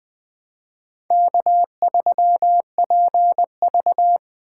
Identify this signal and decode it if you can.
Morse code, heard as a keyed tone, about 20 wpm, 700 Hz. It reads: K3PV